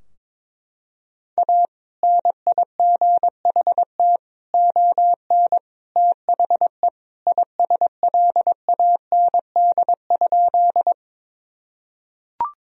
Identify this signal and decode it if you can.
Morse code, heard as a keyed tone, about 22 wpm, 700 Hz. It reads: A NIGHT ON THE ISLAND?
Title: Sentences from Top 1000 Words - Review Part 3 - 22wpm